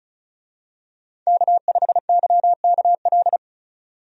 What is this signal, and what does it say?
Morse code, heard as a keyed tone, about 36 wpm, 700 Hz. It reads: K5YKL
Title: USA Call Signs - 5 Characters - 35wpm